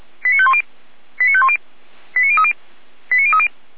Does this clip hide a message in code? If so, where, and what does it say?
Yes: five-tone selective call (ZVEI) from 0.2 to 3.5 s, heard as quick runs of five tones, tones of about 70 ms. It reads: 89510 89510 89030 89030